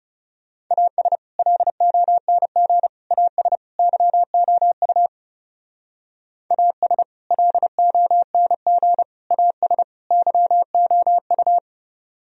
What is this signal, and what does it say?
Morse code, heard as a keyed tone, about 36 wpm, 690 Hz. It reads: AS LONG AS YOU AS LONG AS YOU
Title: Speed-Racing - Sets of 4 Words 30wpm